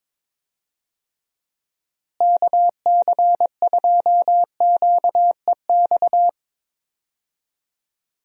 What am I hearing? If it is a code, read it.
Morse code, heard as a keyed tone, about 22 wpm, 700 Hz. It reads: KC2QEX
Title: USA Call Signs - 6 Characters - Repeated 2x - 22wpm